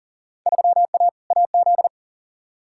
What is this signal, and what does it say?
Morse code, heard as a keyed tone, about 40 wpm, 700 Hz. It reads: 3A AZ
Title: Field Day Exchanges - 40wpm